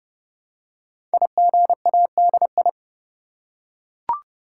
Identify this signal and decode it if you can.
Morse code, heard as a keyed tone, about 30 wpm, 700 Hz. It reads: IGADI
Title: Code Groups - 5 Letters - 30wpm